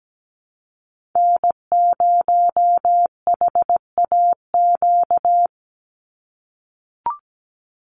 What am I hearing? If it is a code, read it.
Morse code, heard as a keyed tone, about 17 wpm, 700 Hz. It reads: N0HAQ
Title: USA Call Signs 17wpm